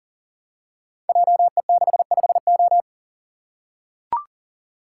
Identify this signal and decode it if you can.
Morse code, heard as a keyed tone, about 38 wpm, 700 Hz. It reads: JE65O
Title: Code Groups - 5 Letter:Numbers - 40wpm